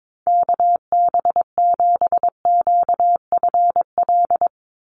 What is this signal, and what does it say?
Morse code, heard as a keyed tone, about 22 wpm, 700 Hz. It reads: KB7QFL
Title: USA Call Signs 22wpm